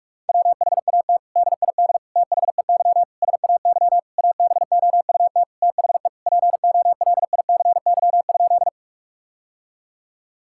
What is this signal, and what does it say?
Morse code, heard as a keyed tone, about 46 wpm, 700 Hz. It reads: WHAT DID THEY SAY ABOUT THE POLICY?
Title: Sentences from Top 1000 Words - Review Part 5 - 45wpm